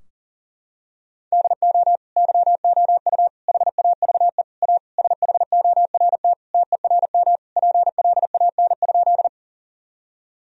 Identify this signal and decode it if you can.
Morse code, heard as a keyed tone, about 40 wpm, 700 Hz. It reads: DO YOU HAVE A SHORT TERM PLAN?
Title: Sentences from Top 1000 Words - Review Part 4 - 40wpm